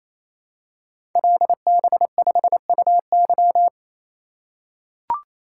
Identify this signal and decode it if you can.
Morse code, heard as a keyed tone, about 29 wpm, 700 Hz. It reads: LB5UY
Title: Code Groups - 5 Letter:Numbers - 28wpm